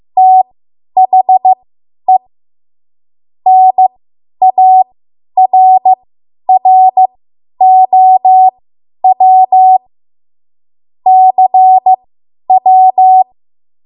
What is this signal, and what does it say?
Morse code, heard as a keyed tone, about 15 wpm, spaced as for 10 wpm, 750 Hz. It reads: THE NARROW CW
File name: Morse / 10WPM